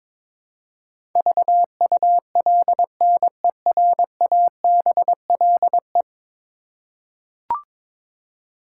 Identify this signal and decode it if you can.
Morse code, heard as a keyed tone, about 22 wpm, 700 Hz. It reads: VULNERABLE
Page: morse-code-ninja-practice-22wpm